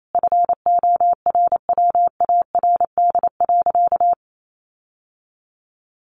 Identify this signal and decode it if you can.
Morse code, heard as a keyed tone, about 28 wpm, 700 Hz. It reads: FORWARD.
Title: Sentences from Top 1300 Words 28wpm